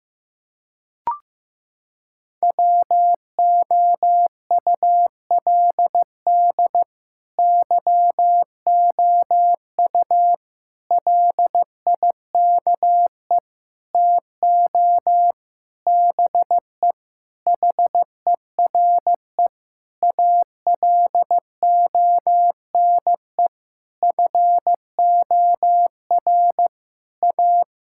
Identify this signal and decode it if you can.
Morse code, heard as a keyed tone, about 15 wpm, 700 Hz. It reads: WOULD YOU LIKE TO BE HERE ALONE FOR A
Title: Sentences from Top 1000 Words - Review Part 5 - 15wpm